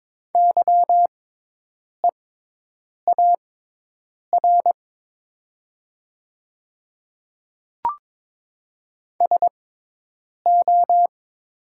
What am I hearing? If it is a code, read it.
Morse code, heard as a keyed tone, about 22 wpm, spaced as for 8 wpm, 700 Hz. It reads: YEAR SO
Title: 6x Character Spacing - Top 200 Words - 22wpm